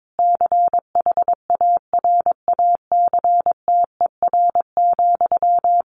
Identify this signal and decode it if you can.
Morse code, heard as a keyed tone, about 22 wpm, 700 Hz. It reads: CHARACTER,
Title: Sentences from Top 1400 Words 22wpm